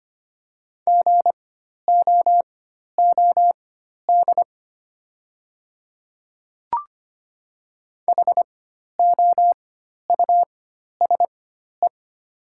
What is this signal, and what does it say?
Morse code, heard as a keyed tone, about 25 wpm, spaced as for 12 wpm, 700 Hz. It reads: GOOD HOUSE
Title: 4x Character Spacing - Top 300 Words - 25wpm